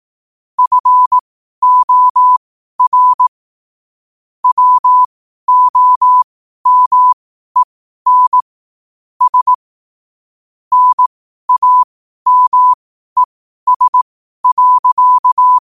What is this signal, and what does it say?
Morse code, heard as a keyed tone, about 18 wpm, spaced as for 12 wpm, 1000 Hz. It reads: FOR WOMEN S NAMES.
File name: qotdfiles / 12wpm